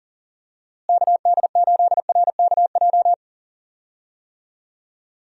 Morse code, heard as a keyed tone, about 40 wpm, 700 Hz. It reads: KD8RKJ
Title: USA Call Signs - 6 Characters - Repeated 3x - 40wpm